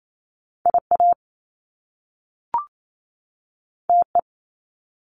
Morse code, heard as a keyed tone, about 28 wpm, 700 Hz. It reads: IA TE
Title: Top Two Letter Combinations in Words 28wpm